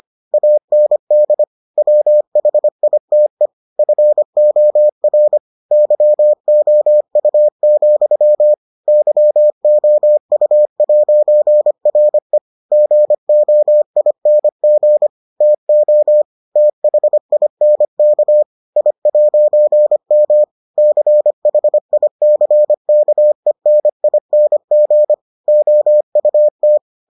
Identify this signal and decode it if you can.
Morse code, heard as a keyed tone, about 25 wpm, 600 Hz. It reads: AND WHITE FOR YOU, YOU'RE GOING TO THINK I'M CHICKENING OUT